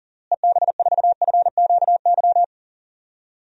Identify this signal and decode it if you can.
Morse code, heard as a keyed tone, about 40 wpm, 700 Hz. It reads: EB4FQY
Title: DXCC-100 Call Signs - 6 Characters - Repeated 3x - 40wpm